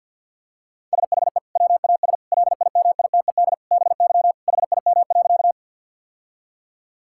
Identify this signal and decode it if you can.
Morse code, heard as a keyed tone, about 50 wpm, 700 Hz. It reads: SHE WAS LIMITED BY HIM.
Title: Sentences from Top 1000 Words - Review Part 6 - 50wpm